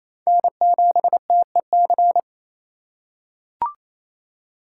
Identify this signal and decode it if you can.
Morse code, heard as a keyed tone, about 28 wpm, 700 Hz. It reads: N7TEC
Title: USA Call Signs - 5 Characters - 2x Character Spacing - 28wpm